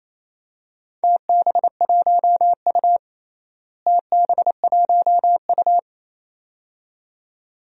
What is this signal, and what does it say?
Morse code, heard as a keyed tone, about 28 wpm, 700 Hz. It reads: TB1U TB1U
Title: DXCC-100 Call Signs - 4 Characters - Repeated 2x - 28wpm